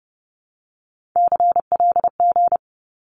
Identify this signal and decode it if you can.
Morse code, heard as a keyed tone, about 30 wpm, 700 Hz. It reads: CLG
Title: Abbreviations 30wpm